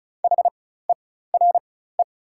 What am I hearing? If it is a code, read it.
Morse code, heard as a keyed tone, about 36 wpm, spaced as for 16 wpm, 700 Hz. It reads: HERE
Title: 4x Character Spacing - Top 200 Words - 35wpm